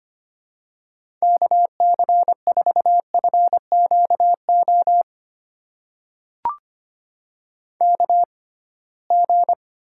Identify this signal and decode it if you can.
Morse code, heard as a keyed tone, about 25 wpm, spaced as for 19 wpm, 700 Hz. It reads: KC4FQO KG
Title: USA Call Signs - 6 Characters - 6x Character Spacing - 25wpm